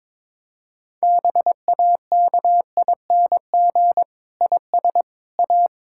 Morse code, heard as a keyed tone, about 22 wpm, 700 Hz. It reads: BAKING IS A
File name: Difficult Sentences with Hints - 22wpm